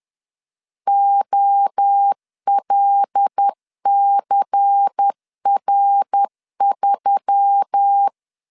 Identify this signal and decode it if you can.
Morse code, heard as a keyed tone, about 10 wpm, 800 Hz. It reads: OLCR3
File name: LEZIONI / 6 PKX0 / 6 P 10wpm 2432